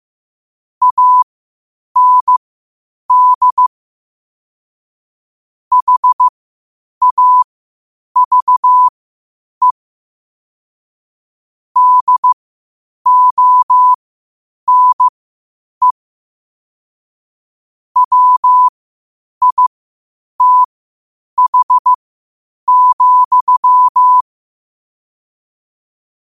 Morse code, heard as a keyed tone, about 15 wpm, spaced as for 8 wpm, 1000 Hz. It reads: AND HAVE DONE WITH,
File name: qotdfiles / 8wpm